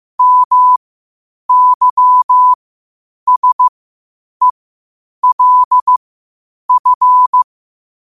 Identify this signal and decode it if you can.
Morse code, heard as a keyed tone, about 15 wpm, spaced as for 8 wpm, 1000 Hz. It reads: MYSELF
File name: qotdfiles / 8wpm